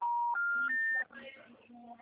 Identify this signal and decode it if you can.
Special information tone, with speech.
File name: out- -2002- - - .71